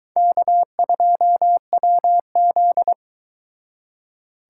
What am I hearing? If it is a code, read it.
Morse code, heard as a keyed tone, about 23 wpm, 700 Hz. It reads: K2WZ